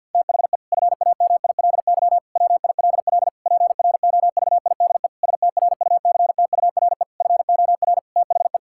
Morse code, heard as a keyed tone, about 50 wpm, 700 Hz. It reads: THE FAMILY WILL PROVIDE STRUCTURE FOR THE
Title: Sentences from Top 1000 Words - Review Part 6 - 50wpm